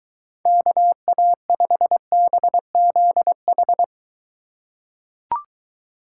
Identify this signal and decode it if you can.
Morse code, heard as a keyed tone, about 21 wpm, 700 Hz. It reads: KA5BZH